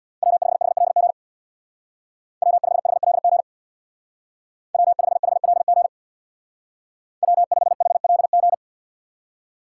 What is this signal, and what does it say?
Morse code, heard as a keyed tone, about 64 wpm, 700 Hz. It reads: W5HLG W5HLG W5HLG W5HLG